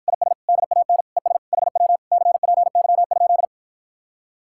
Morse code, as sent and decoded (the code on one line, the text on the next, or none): .. ... / -.. .- -. / . ... / .... .-- / -.-. .--. -.-- ..--..
IS DAN ES HW CPY?